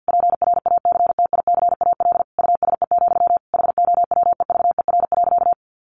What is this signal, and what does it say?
Morse code, heard as a keyed tone, about 50 wpm, 700 Hz. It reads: PRACTICAL USE, HOWEVER.
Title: Sentences from Top 1400 Words 50wpm